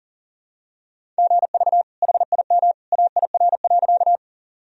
Morse code, heard as a keyed tone, about 40 wpm, 700 Hz. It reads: GV HIM AIR.
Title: Sentences from Top 200 Words and Abbreviations 40wpm